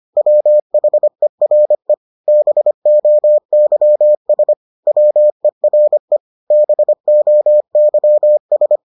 Morse code, heard as a keyed tone, about 25 wpm, 600 Hz. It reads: WHERE BOYS WERE BOYS